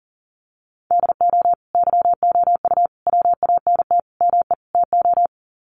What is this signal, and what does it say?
Morse code, heard as a keyed tone, about 40 wpm, 700 Hz. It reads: DO YOU WANT ME TO
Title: Sets of 5 Words 40wpm